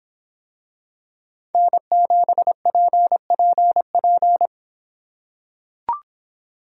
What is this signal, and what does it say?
Morse code, heard as a keyed tone, about 24 wpm, 700 Hz. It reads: N7PPP